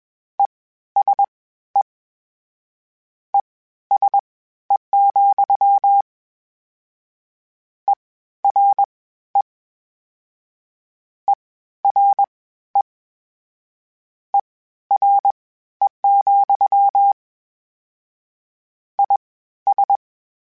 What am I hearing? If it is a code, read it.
Morse code, heard as a keyed tone, about 21 wpm, spaced as for 13 wpm, 800 Hz. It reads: ESE ESE, ERE ERE ERE, IS
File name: General / General-mp3s / Suffixes sp @20